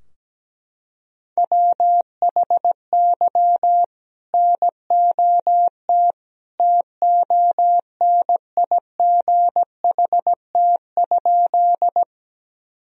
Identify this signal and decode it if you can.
Morse code, heard as a keyed tone, about 17 wpm, 700 Hz. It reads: WHY NOT TONIGHT?
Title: Sentences from Top 1000 Words - Review Part 3 - 17wpm